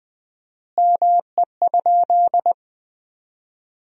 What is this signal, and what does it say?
Morse code, heard as a keyed tone, about 20 wpm, 700 Hz. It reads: ME?
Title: 4x Word-Spacing - Sentences from Top 200 Words - 20wpm